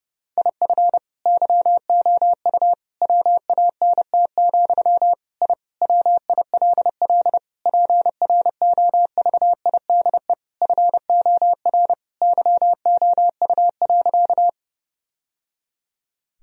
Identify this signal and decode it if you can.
Morse code, heard as a keyed tone, about 30 wpm, 700 Hz. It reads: IF YOU WANT, I WILL PROVIDE FOR YOU.
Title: Sentences from Top 1000 Words - Review Part 3 - 30wpm